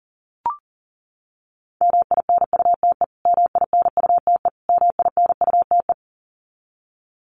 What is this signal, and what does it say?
Morse code, heard as a keyed tone, about 40 wpm, 700 Hz. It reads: MINUTE MINUTE MINUTE
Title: Top 500 Words Repeated 40wpm